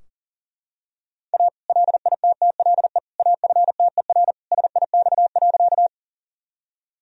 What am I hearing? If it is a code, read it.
Morse code, heard as a keyed tone, about 40 wpm, 700 Hz. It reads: A LITTLE AFTER SIX.
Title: Sentences from Top 1000 Words - Review Part 2 - 40wpm